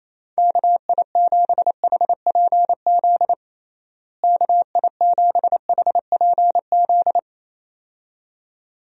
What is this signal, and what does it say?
Morse code, heard as a keyed tone, about 28 wpm, 700 Hz. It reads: KI7HPZ KI7HPZ
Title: USA Call Signs - 6 Characters - Repeated 2x - 28wpm